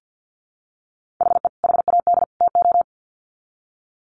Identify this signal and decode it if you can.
Morse code, heard as a keyed tone, about 50 wpm, 700 Hz. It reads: HE HAD TO
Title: Sets of 3 Words 50wpm